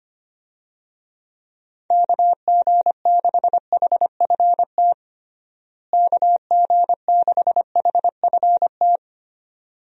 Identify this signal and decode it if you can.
Morse code, heard as a keyed tone, about 25 wpm, 700 Hz. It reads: KG6HFT KG6HFT
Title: USA Call Signs - 6 Characters - Repeated 3x - 25wpm